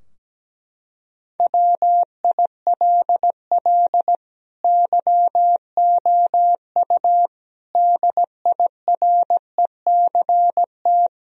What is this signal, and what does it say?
Morse code, heard as a keyed tone, about 17 wpm, 700 Hz. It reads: WILL YOU DIRECT
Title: Sentences from Top 1000 Words - Review Part 2 - 17wpm